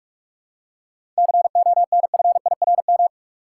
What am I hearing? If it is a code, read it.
Morse code, heard as a keyed tone, about 43 wpm, 700 Hz. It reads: CONFIRM
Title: Seven Letter Words - 45wpm